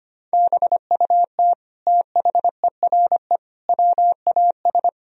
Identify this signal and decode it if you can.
Morse code, heard as a keyed tone, about 25 wpm, 700 Hz. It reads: BUT THERE WAS
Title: Sentences from Top 1000 Words - Review Part 3 - 25wpm